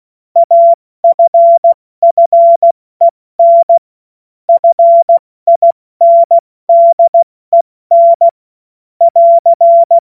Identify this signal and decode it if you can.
Morse code, heard as a keyed tone, about 16 wpm, 680 Hz. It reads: AFFEN FINDEN <AR>
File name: Morsekurs / Witze / Witz 05 Dezember 2023 70 F80